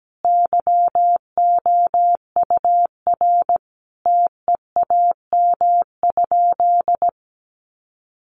Morse code, heard as a keyed tone, about 17 wpm, 700 Hz. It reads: YOUR TEAM?
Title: Sentences from Top 1500 Words 17wpm